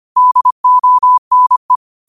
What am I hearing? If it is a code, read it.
Morse code, heard as a keyed tone, about 25 wpm, 1000 Hz. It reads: DONE